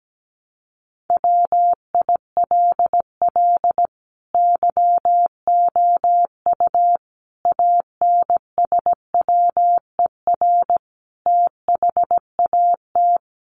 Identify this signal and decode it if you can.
Morse code, heard as a keyed tone, about 17 wpm, 700 Hz. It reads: WILL YOU ANSWER THAT
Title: Sentences from Top 400 Words 17wpm